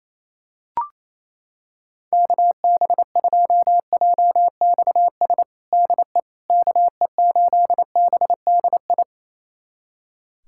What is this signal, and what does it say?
Morse code, heard as a keyed tone, about 28 wpm, 700 Hz. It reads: KB2JXS DE KE8BDI
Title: USA Callsign DE Callsign 28wpm